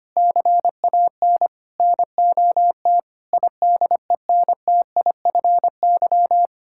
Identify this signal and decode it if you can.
Morse code, heard as a keyed tone, about 25 wpm, 700 Hz. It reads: CAN NOT IDENTIFY
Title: Sentences from Top 1000 Words - Review Part 6 - 25wpm